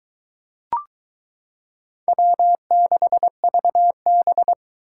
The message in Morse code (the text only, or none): W6VB